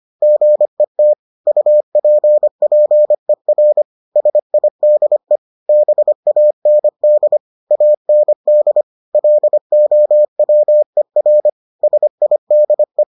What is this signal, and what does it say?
Morse code, heard as a keyed tone, about 25 wpm, 600 Hz. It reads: GET UPPER SIDE BAND AND LOWER SIDE